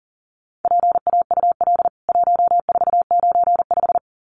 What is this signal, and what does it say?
Morse code, heard as a keyed tone, about 40 wpm, 700 Hz. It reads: PAUL 1495
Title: CWT Exchange 40wpm